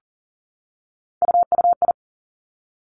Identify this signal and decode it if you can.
Morse code, heard as a keyed tone, about 38 wpm, 700 Hz. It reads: UUI
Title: Three Challenging Letter-Number Combinations 40wpm